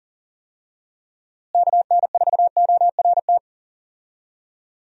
Morse code, heard as a keyed tone, about 40 wpm, 700 Hz. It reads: KN4ORT
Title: USA Call Signs - 6 Characters - Repeated 2x - 40wpm